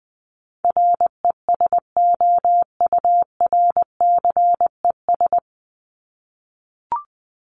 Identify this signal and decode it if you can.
Morse code, heard as a keyed tone, about 20 wpm, 700 Hz. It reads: RESOURCES
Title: ES:S Suffix - Top 100 Words with Base Words 20wpm